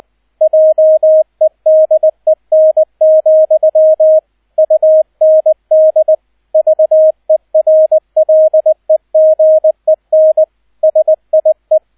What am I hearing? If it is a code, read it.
Morse code, heard as a keyed tone, about 20 wpm, 630 Hz, with radio noise. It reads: JEDEN, UND VERLEGEN SIE
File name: Erster Streich 20wpm